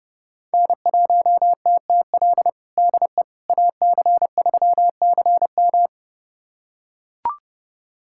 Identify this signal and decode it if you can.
Morse code, heard as a keyed tone, about 30 wpm, 700 Hz. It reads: N1TTL DE AC3CM
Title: USA Callsign DE Callsign - 5 Characters - 30wpm